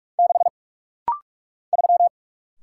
Morse code, heard as a keyed tone, about 47 wpm, 700 Hz. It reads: 6 3
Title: Single Letter-Number - Rapid-Fire 45wpm